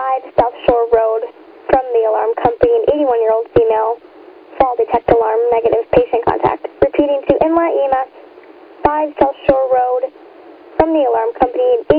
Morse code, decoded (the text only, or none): none